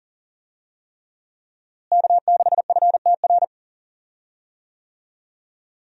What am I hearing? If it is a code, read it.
Morse code, heard as a keyed tone, about 39 wpm, 700 Hz. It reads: K6FTR
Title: USA Call Signs - 5 Characters - Repeated 3x - 40wpm